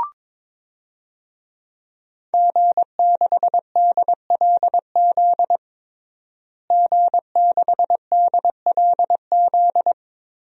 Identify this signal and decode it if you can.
Morse code, heard as a keyed tone, about 22 wpm, 700 Hz. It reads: G6DLZ G6DLZ
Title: DXCC-100 Call Signs - 5 Characters - Repeated 3x - 22wpm